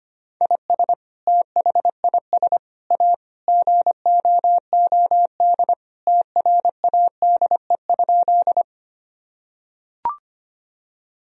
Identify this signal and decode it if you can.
Morse code, heard as a keyed tone, about 25 wpm, 700 Hz. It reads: IS THIS A GOOD TRADE?
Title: Sentences from Top 1000 Words - Review Part 4 - 25wpm